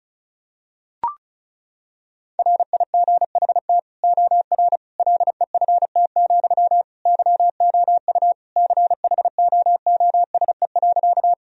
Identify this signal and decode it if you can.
Morse code, heard as a keyed tone, about 36 wpm, 700 Hz. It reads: RIGHT OR LEFT, YOU CHOOSE.
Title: Sentences from Top 1000 Words - Review Part 4 - 35wpm